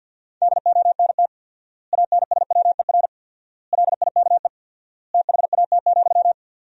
Morse code, heard as a keyed tone, about 51 wpm, 700 Hz. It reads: DONT ANSWER LIKE THAT,
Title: 4x Word-Spacing - Sentences from Top 400 Words - 50wpm